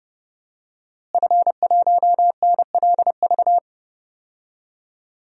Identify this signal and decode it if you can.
Morse code, heard as a keyed tone, about 32 wpm, 700 Hz. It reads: F1NLV
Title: DXCC-100 Call Signs - 5 Characters - Repeated 3x - 30wpm